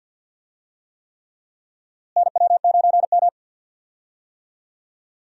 Morse code, heard as a keyed tone, about 46 wpm, 700 Hz. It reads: NW9M
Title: USA Call Signs - 4 Characters - Repeated 3x - 50wpm